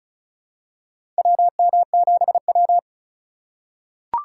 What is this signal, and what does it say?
Morse code, heard as a keyed tone, about 36 wpm, 700 Hz. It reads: WM7W
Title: USA Call Signs - 4 Characters - 4x Character Spacing - 35wpm